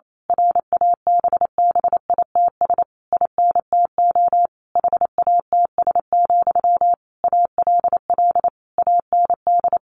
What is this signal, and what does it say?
Morse code, heard as a keyed tone, about 28 wpm, 700 Hz. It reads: RABBITS INTO HATS, ALL AND
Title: A Sound Of Thunder 28wpm